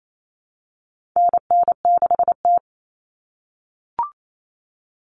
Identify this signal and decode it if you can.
Morse code, heard as a keyed tone, about 28 wpm, 700 Hz. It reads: NN6T
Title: Call Signs - 28wpm